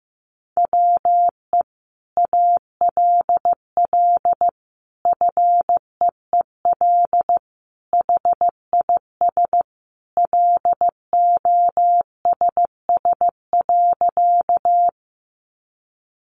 Morse code, heard as a keyed tone, about 15 wpm, 700 Hz. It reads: WE ALL FEEL HIS LOSS.
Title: Sentences from Top 1400 Words 15wpm